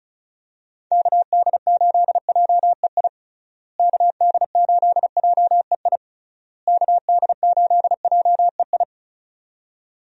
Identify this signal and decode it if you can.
Morse code, heard as a keyed tone, about 34 wpm, 700 Hz. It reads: KD8JEI KD8JEI KD8JEI
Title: USA Call Signs - 6 Characters - Repeated 3x - 35wpm